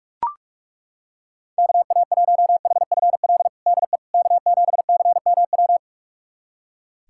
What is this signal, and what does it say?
Morse code, heard as a keyed tone, about 42 wpm, 700 Hz. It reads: KA1HFL DE K7CMW
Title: USA Callsign DE Callsign 45wpm